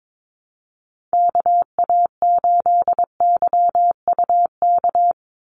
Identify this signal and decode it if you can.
Morse code, heard as a keyed tone, about 22 wpm, 700 Hz. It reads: KA8YUK
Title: USA Call Signs 22wpm